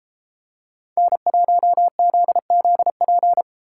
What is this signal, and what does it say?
Morse code, heard as a keyed tone, about 31 wpm, 700 Hz. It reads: N1ZZP